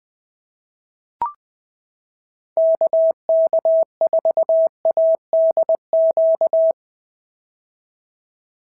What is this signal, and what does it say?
Morse code, heard as a keyed tone, about 20 wpm, 650 Hz. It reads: KK4ADQ
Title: POTA QSO Practice - 20wpm